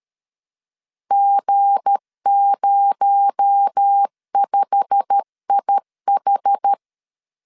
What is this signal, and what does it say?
Morse code, heard as a keyed tone, about 13 wpm, 800 Hz. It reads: G05IH
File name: LEZIONI / 9 2867 12wpm 2322